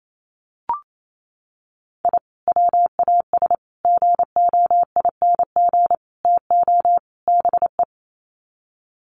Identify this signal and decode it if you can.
Morse code, heard as a keyed tone, about 28 wpm, 700 Hz. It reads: I WAS GOING TO BE
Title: Sets of 5 Words 28wpm